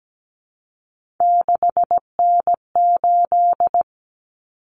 Morse code, heard as a keyed tone, about 17 wpm, 700 Hz. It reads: 6N8
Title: Common Cut Numbers - 3-Digit 17wpm